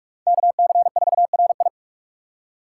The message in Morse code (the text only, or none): KC4RI